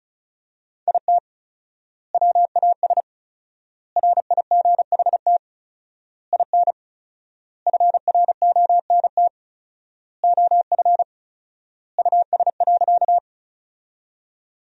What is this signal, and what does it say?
Morse code, heard as a keyed tone, about 34 wpm, 700 Hz. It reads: IT WAS RIGHT IN FRONT OF US.
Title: 4x Word-Spacing - Sentences from Top 500 Words - 35wpm